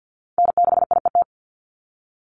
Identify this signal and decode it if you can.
Morse code, heard as a keyed tone, about 50 wpm, 700 Hz. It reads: N6IET
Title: Call Signs Repeated 2x - 50wpm